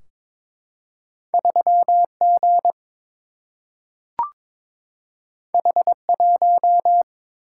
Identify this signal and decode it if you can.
Morse code, heard as a keyed tone, about 22 wpm, 700 Hz. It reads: 3 G H 1
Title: Single-Letter Single-Number 22wpm